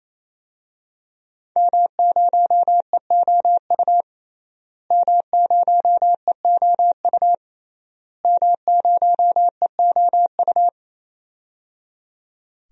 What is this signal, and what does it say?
Morse code, heard as a keyed tone, about 28 wpm, 700 Hz. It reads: M0EOU M0EOU M0EOU
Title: DXCC-100 Call Signs - 5 Characters - Repeated 3x - 28wpm